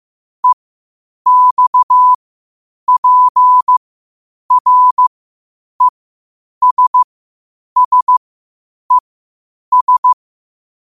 Morse code, heard as a keyed tone, about 15 wpm, spaced as for 8 wpm, 1000 Hz. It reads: EXPRESSES
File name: qotdfiles / 8wpm